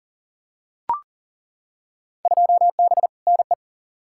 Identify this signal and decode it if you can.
Morse code, heard as a keyed tone, about 40 wpm, 700 Hz. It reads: 2B NE
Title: Field Day Exchanges Encore - 40wpm